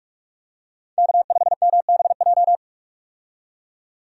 Morse code, heard as a keyed tone, about 46 wpm, 700 Hz. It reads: K5MBJ
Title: USA Call Signs - 5 Characters - 45wpm